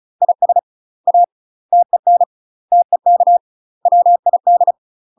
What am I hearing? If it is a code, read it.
Morse code, heard as a keyed tone, about 34 wpm, 700 Hz. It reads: IS A TEN TEK WID
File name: UK MASTER QSOs TRAINING / 35WPM / UK MASTER QSOs 0040